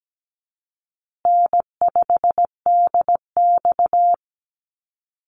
Morse code, heard as a keyed tone, about 17 wpm, 700 Hz. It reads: N5DX
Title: Easy Call Signs Repeated 3x - 17wpm